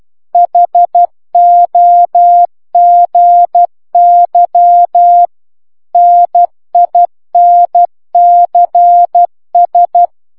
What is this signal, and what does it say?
Morse code, heard as a keyed tone, about 12 wpm, 700 Hz. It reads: HOGY NINCS